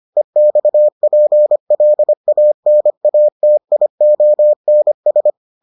Morse code, heard as a keyed tone, about 25 wpm, 600 Hz. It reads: EXPLANATIONS